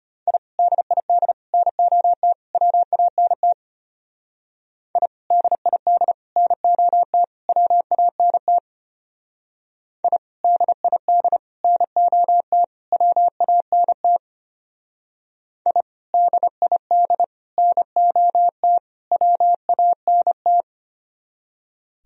Morse code, heard as a keyed tone, about 29 wpm, 700 Hz. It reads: I DID NOT WANT I DID NOT WANT I DID NOT WANT I DID NOT WANT
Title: Speed-Racing - Sets of 4 Words 25wpm